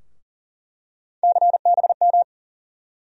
Morse code, heard as a keyed tone, about 40 wpm, 700 Hz. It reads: CBM
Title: USA Call Sign Postfixes - 3 Letters 40wpm